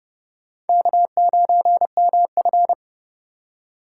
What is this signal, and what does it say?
Morse code, heard as a keyed tone, about 29 wpm, 700 Hz. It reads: K9MF